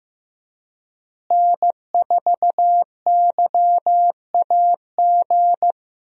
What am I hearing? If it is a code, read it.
Morse code, heard as a keyed tone, about 15 wpm, 700 Hz. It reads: N4YAG
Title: USA Call Signs - 5 Characters - Repeated 3x - 15wpm